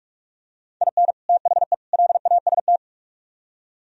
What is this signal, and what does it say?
Morse code, heard as a keyed tone, about 45 wpm, 700 Hz. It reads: IN THE LAST